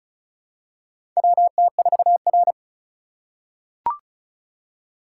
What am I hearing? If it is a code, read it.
Morse code, heard as a keyed tone, about 36 wpm, 700 Hz. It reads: WT4R
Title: USA Call Signs - 4 Characters - 2x Character Spacing - 35wpm